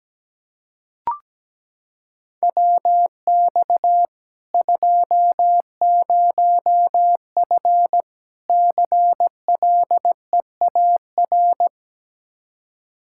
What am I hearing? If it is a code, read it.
Morse code, heard as a keyed tone, about 17 wpm, 700 Hz. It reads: WX 20F CLEAR
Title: K1USN SST QSO Practice 17wpm